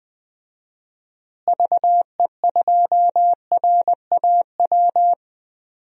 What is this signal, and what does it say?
Morse code, heard as a keyed tone, about 20 wpm, 700 Hz. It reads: VE2RAW
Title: DXCC-100 Call Signs - 6 Characters - Repeated 3x - 20wpm